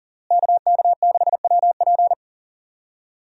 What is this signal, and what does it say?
Morse code, heard as a keyed tone, about 37 wpm, 700 Hz. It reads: KK6WP
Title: USA Call Signs - 5 Characters - Repeated 3x - 40wpm